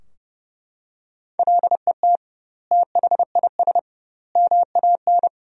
Morse code, heard as a keyed tone, about 32 wpm, 700 Hz. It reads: LET THIS MAN
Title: 2x Word-Spacing - Sentences from Top 400 Words - 30wpm